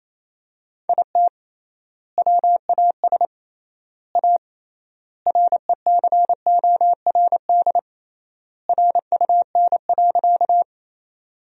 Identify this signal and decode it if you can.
Morse code, heard as a keyed tone, about 28 wpm, 700 Hz. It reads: IT WAS A RECORD RUN.
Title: 3x Word-Spacing - Sentences from Top 500 Words - 28wpm